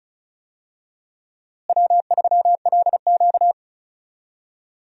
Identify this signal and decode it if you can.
Morse code, heard as a keyed tone, about 36 wpm, 700 Hz. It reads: W3LQ
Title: USA Call Signs - 4 Characters - Repeated 2x - 35wpm